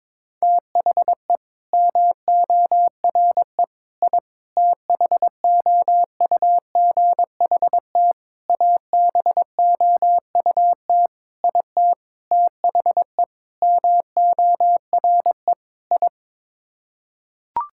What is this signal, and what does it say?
Morse code, heard as a keyed tone, about 22 wpm, 700 Hz. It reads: THE MORE I THOUGHT ABOUT IT THE MORE I
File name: Sets of 9 Words - 22wpm